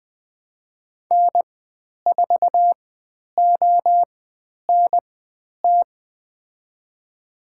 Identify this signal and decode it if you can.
Morse code, heard as a keyed tone, about 20 wpm, spaced as for 10 wpm, 700 Hz. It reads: N4ONT
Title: 109 Lesson 18g - Letter m - Call Signs 20wpm